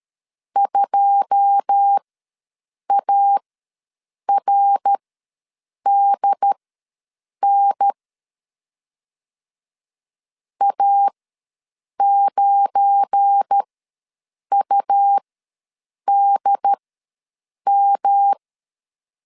Morse code, heard as a keyed tone, about 13 wpm, spaced as for 7 wpm, 800 Hz. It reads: 2ARDN A9UDM
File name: LEZIONI / 9 2 ws 12wpm 3406